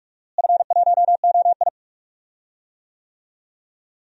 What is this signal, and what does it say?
Morse code, heard as a keyed tone, about 43 wpm, 700 Hz. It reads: F1OI
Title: DXCC-100 Call Signs - 4 Characters - Repeated 3x - 45wpm